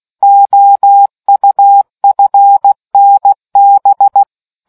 Morse code, heard as a keyed tone, about 16 wpm, 800 Hz. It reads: OUFNB